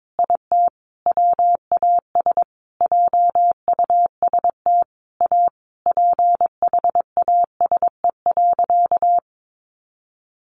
Morse code, heard as a keyed tone, about 22 wpm, 700 Hz. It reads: IT WAS JUST A PHASE.
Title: Sentences from Top 1200 Words 22wpm